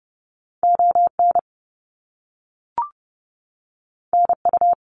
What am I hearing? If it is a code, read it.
Morse code, heard as a keyed tone, about 30 wpm, 700 Hz. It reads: ON NU
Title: Canadian Provinces and Territories 30wpm